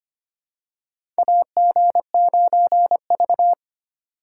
Morse code, heard as a keyed tone, about 25 wpm, 700 Hz. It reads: AG9V
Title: USA Call Signs - 4 Characters - 3x Character Spacing - 25wpm